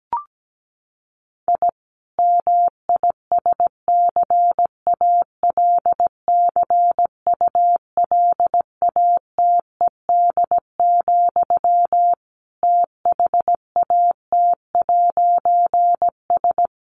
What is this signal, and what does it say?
Morse code, heard as a keyed tone, about 17 wpm, 700 Hz. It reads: I MISCALCULATED, THAT'S
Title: A Sound Of Thunder 17wpm